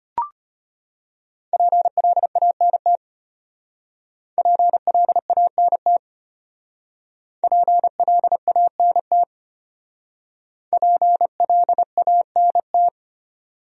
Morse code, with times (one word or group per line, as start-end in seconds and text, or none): 1.5-3.0 PLANT
4.4-6.0 PLANT
7.4-9.2 PLANT
10.7-12.9 PLANT